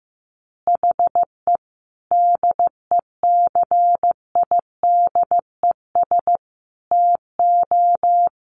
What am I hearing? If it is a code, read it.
Morse code, heard as a keyed tone, about 15 wpm, 700 Hz. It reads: HE DECIDES TO
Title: Additional Easy Phrases 15wpm